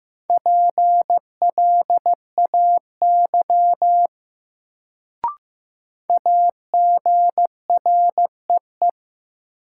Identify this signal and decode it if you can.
Morse code, heard as a keyed tone, about 15 wpm, 700 Hz. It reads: PLAY AGREE
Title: Sets of 5 Words Encore - New Word Drill - 15wpm